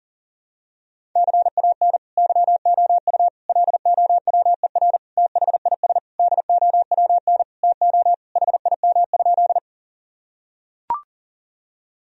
Morse code, heard as a keyed tone, about 40 wpm, 700 Hz. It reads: CAN YOU LOWER THIS DOWN TO HIM?
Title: Sentences from Top 1000 Words - Review Part 4 - 40wpm